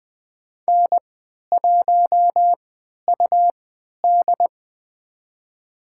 Morse code, heard as a keyed tone, about 20 wpm, spaced as for 11 wpm, 700 Hz. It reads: N1UD
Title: USA Call Signs - 4 Characters - 3x Character Spacing - 20wpm